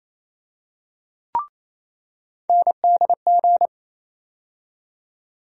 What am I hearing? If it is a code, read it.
Morse code, heard as a keyed tone, about 28 wpm, 700 Hz. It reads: NDG
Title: USA Call Sign Postfixes - 3 Letters 28wpm